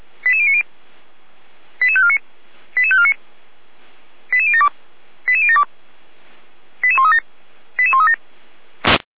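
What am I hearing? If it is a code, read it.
Five-tone selective call (ZVEI), heard as quick runs of five tones, tones of about 70 ms. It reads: 88090 88549 88549 88072 88072 88137 88137
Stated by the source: ZVEI five-tone call received by SDR, archived on the Feuerwehr Orlamünde audio server (f85360).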